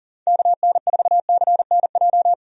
Morse code, heard as a keyed tone, about 40 wpm, 700 Hz. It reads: KN4CNJ
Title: USA Call Signs - 6 Characters - 40wpm